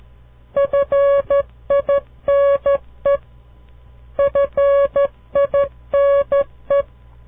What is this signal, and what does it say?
Morse code, heard as a keyed tone, about 13 wpm, 560 Hz. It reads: FINE FINE
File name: CW Learning Programs / farnsworth / unit 2